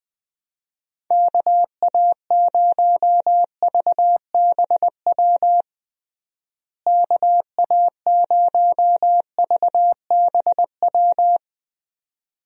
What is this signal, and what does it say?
Morse code, heard as a keyed tone, about 21 wpm, 700 Hz. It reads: KA0VBW KA0VBW
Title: USA Call Signs - 6 Characters - Repeated 2x - 20wpm